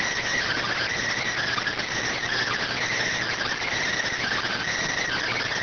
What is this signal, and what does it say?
Emergency Alert System (SAME) tones, heard as harsh buzzing bursts. It is heard from 1.1 to 4.6 s.